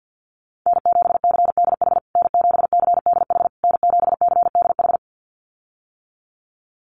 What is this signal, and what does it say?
Morse code, heard as a keyed tone, about 48 wpm, 700 Hz. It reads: N7CDH N7CDH N7CDH
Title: USA Call Signs Repeated 3x 50wpm